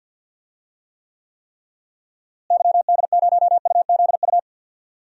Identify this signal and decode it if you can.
Morse code, heard as a keyed tone, about 52 wpm, 700 Hz. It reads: YD0UZU
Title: DXCC-100 Call Signs - 6 Characters - Repeated 3x - 50wpm